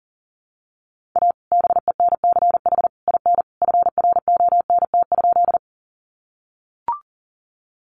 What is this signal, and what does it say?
Morse code, heard as a keyed tone, about 40 wpm, 700 Hz. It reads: A BENCH IN FRONT?
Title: Sentences from Top 1400 Words 40wpm